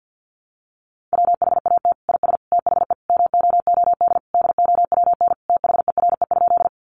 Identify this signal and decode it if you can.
Morse code, heard as a keyed tone, about 50 wpm, 700 Hz. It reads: WHAT IS THE MOOD DOWN THERE?
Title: Sentences from Top 1100 Words 50wpm